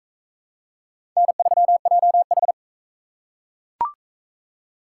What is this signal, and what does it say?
Morse code, heard as a keyed tone, about 40 wpm, 700 Hz. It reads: N3JH